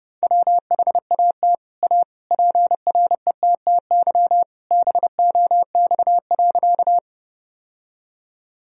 Morse code, heard as a keyed tone, about 30 wpm, 700 Hz. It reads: WHAT A PRETTY BOX.
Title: Sentences from Top 1000 Words - Review Part 2 - 30wpm